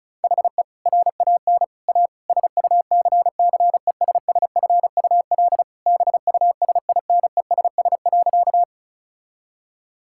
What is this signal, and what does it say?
Morse code, heard as a keyed tone, about 36 wpm, 700 Hz. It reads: HE RAN A SUCCESSFUL BUSINESS.
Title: Sentences from Top 1000 Words - Review Part 6 - 35wpm